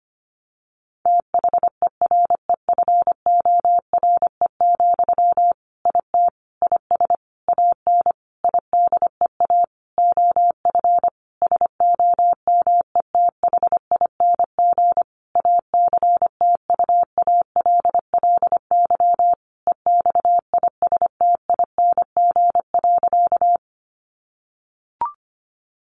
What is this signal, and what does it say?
Morse code, heard as a keyed tone, about 25 wpm, 700 Hz. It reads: THEREFORE, IT IS AN IDEA OF SOMETHING ACTUALLY EXISTING.
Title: Sentences from Top 1500 Words 25wpm